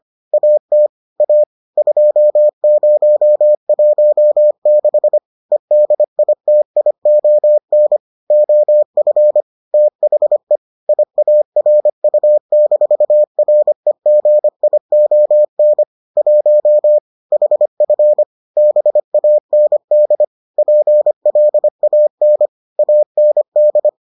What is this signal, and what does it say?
Morse code, heard as a keyed tone, about 25 wpm, 600 Hz. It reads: AT A 2016 EDITION OF THE IARU-REGION 1 HF BAND PLAN AND